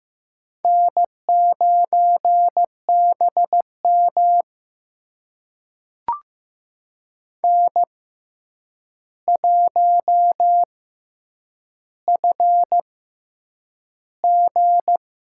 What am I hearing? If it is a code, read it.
Morse code, heard as a keyed tone, about 15 wpm, spaced as for 8 wpm, 700 Hz. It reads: N9BM N1FG
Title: USA Call Signs - 4 Characters - 6x Character Spacing - 15wpm